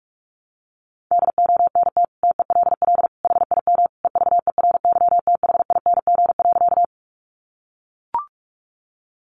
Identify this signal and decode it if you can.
Morse code, heard as a keyed tone, about 44 wpm, 700 Hz. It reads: DONT TELL HIM EVERYTHING.
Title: Sentences from Top 700 Words 45wpm